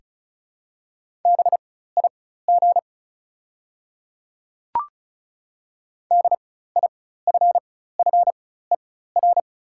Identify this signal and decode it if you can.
Morse code, heard as a keyed tone, about 34 wpm, spaced as for 16 wpm, 690 Hz. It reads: BIG DIFFER
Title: 4x Character Spacing - Top 200 Words - 35wpm